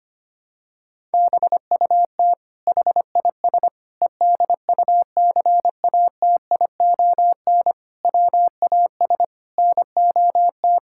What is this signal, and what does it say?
Morse code, heard as a keyed tone, about 25 wpm, 700 Hz. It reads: BUT HIS EDUCATION WAS NOT
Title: Sentences from Top 1000 Words - Review Part 6 - 25wpm